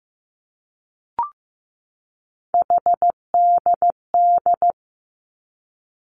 Morse code, heard as a keyed tone, about 16 wpm, 700 Hz. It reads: HDD
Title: Three Challenging Letter-Number Combinations 15wpm